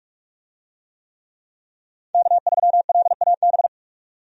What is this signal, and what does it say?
Morse code, heard as a keyed tone, about 46 wpm, 700 Hz. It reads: K3LAB
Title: USA Call Signs - 5 Characters - 45wpm